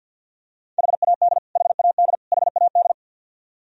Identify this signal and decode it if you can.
Morse code, heard as a keyed tone, about 47 wpm, 700 Hz. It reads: HAD HAD HAD
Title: Top 100 Words Repeated 50wpm